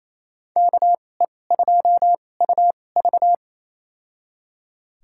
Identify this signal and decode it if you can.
Morse code, heard as a keyed tone, about 28 wpm, spaced as for 20 wpm, 700 Hz. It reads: KE2UV
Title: USA Call Signs - 5 Characters - 2x Character Spacing - 28wpm